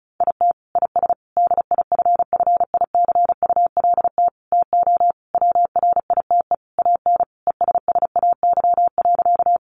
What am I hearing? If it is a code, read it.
Morse code, heard as a keyed tone, about 34 wpm, 700 Hz. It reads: IT IS DIFFICULT TO WRITE AN ESSAY.
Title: Sentences from Top 1500 Words 35wpm